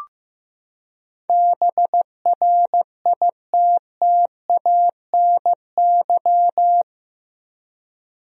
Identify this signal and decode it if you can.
Morse code, heard as a keyed tone, about 15 wpm, 700 Hz. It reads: BRITTANY
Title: USA Common Names - 15wpm